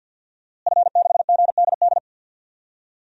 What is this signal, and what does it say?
Morse code, heard as a keyed tone, about 49 wpm, 700 Hz. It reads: F6GDD
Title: DXCC-100 Call Signs - 5 Characters - Repeated 3x - 50wpm